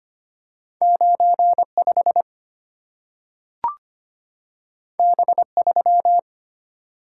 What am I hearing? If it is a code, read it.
Morse code, heard as a keyed tone, about 25 wpm, 690 Hz. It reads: 95 B3
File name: Two Letter-Number Combinations - 25wpm